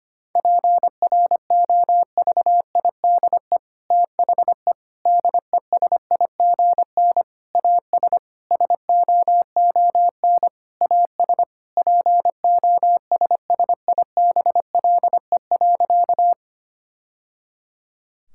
Morse code, heard as a keyed tone, about 25 wpm, 700 Hz. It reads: PROVIDE THE DESIGN AS SOON AS POSSIBLE.
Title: Sentences from Top 1000 Words - Review Part 5 - 25wpm